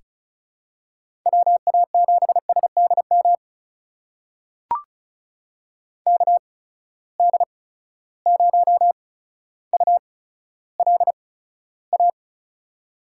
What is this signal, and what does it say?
Morse code, heard as a keyed tone, about 36 wpm, spaced as for 15 wpm, 700 Hz. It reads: WA7SDM KD0ULA
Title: USA Call Signs - 6 Characters - 8x Character Spacing - 35wpm